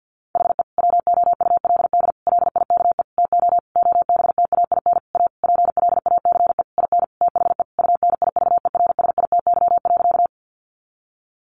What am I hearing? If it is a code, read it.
Morse code, heard as a keyed tone, about 50 wpm, 700 Hz. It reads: HE WOULD LIKE TO OBTAIN A PLACE IN THE UNIVERSITY.
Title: Sentences from Top 1300 Words 50wpm